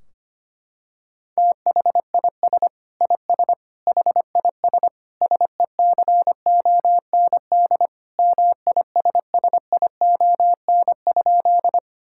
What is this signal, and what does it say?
Morse code, heard as a keyed tone, about 25 wpm, 700 Hz. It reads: THIS IS HIS SECOND MISSION?
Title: Sentences from Top 1600 Words 25wpm